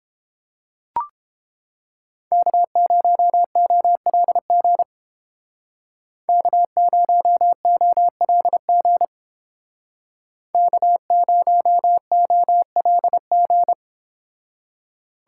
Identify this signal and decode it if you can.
Morse code, heard as a keyed tone, about 29 wpm, 700 Hz. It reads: K0OLG K0OLG K0OLG